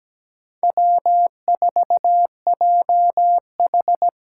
Morse code, heard as a keyed tone, about 17 wpm, 700 Hz. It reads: W4JH
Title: USA Call Signs - 5 Characters - 8x Character Spacing - 17wpm